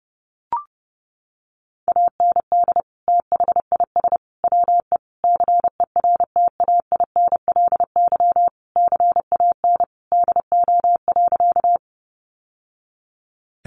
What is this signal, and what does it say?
Morse code, heard as a keyed tone, about 30 wpm, 700 Hz. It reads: AND THIS WE CERTAINLY CAN DO.
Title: Sentences from Top 900 Words 30wpm